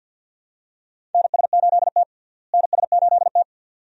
Morse code, heard as a keyed tone, about 48 wpm, 690 Hz. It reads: NS8T NS8T
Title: USA Call Signs - 4 Characters - Repeated 2x - 50wpm